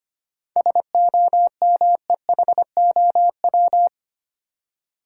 Morse code, heard as a keyed tone, about 25 wpm, 700 Hz. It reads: SOMEHOW
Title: Seven Letter Words - 25wpm